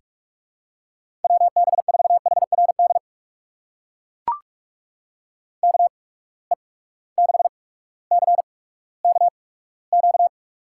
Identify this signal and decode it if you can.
Morse code, heard as a keyed tone, about 43 wpm, spaced as for 19 wpm, 700 Hz. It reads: WB4HRD KE6CKQ
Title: USA Call Signs - 6 Characters - 8x Character Spacing - 45wpm